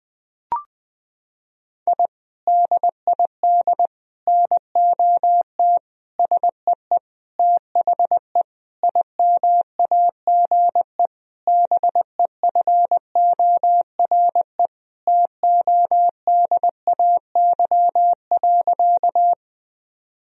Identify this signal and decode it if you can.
Morse code, heard as a keyed tone, about 20 wpm, 700 Hz. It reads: I DID NOT SEE THE IMAGE BEFORE TODAY.
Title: Sentences from Top 1000 Words - Review Part 4 - 20wpm